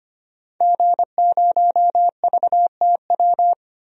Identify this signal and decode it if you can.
Morse code, heard as a keyed tone, about 25 wpm, 700 Hz. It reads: G0VTW
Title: DXCC-100 Call Signs - 5 Characters - 25wpm